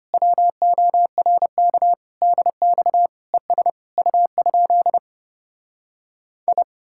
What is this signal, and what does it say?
Morse code, heard as a keyed tone, about 33 wpm, 700 Hz. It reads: WORK DX ES U? I